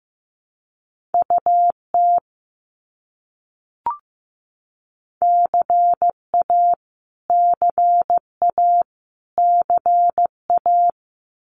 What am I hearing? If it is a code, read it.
Morse code, heard as a keyed tone, about 15 wpm, 700 Hz. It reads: UT CA CA CA
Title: US States Repeated 3x 15wpm